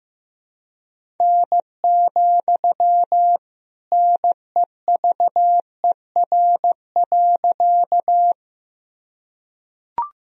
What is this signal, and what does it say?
Morse code, heard as a keyed tone, about 16 wpm, 700 Hz. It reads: N, NEVER.
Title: Sentences from Top 300 Words and Abbreviations 15wpm